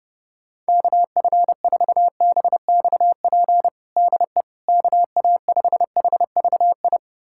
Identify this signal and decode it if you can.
Morse code, heard as a keyed tone, about 31 wpm, 700 Hz. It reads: KF4BXP DE KA5HVI
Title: USA Callsign DE Callsign - 6 Characters - 30wpm